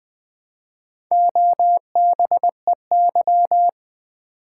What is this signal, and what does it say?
Morse code, heard as a keyed tone, about 20 wpm, 700 Hz. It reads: OBEY
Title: Build A Word - 1 Letter At A Time - 4-Letter Words - 20wpm